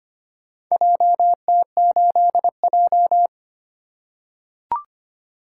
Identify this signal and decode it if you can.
Morse code, heard as a keyed tone, about 25 wpm, 700 Hz. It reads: JT8J
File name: practice / 25wpm / Four Letter-Number Combinations - 25wpm